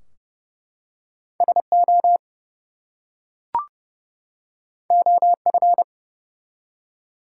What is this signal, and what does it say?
Morse code, heard as a keyed tone, about 31 wpm, 700 Hz. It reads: SO OF
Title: Two Letter Words - 30wpm